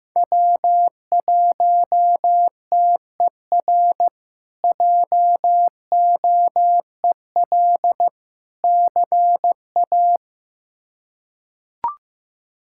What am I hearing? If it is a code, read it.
Morse code, heard as a keyed tone, about 15 wpm, 700 Hz. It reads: W1TER JOEL CA
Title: K1USN SST QSO Practice - 15wpm